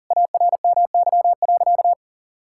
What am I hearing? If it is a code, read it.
Morse code, heard as a keyed tone, about 37 wpm, 700 Hz. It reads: ARMY.
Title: Sentences from Top 1000 Words - Review Part 5 - 40wpm